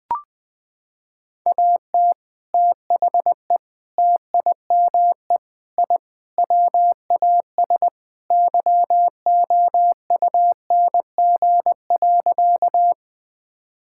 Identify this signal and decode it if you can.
Morse code, heard as a keyed tone, about 20 wpm, 700 Hz. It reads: AT THE TIME I WAS YOUNG.
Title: Sentences with 3-Word Phrases - 20wpm